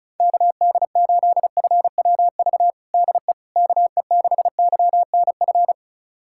Morse code, heard as a keyed tone, about 34 wpm, 700 Hz. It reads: KD8FWV DE KE6YNF
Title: USA Callsign DE Callsign 35wpm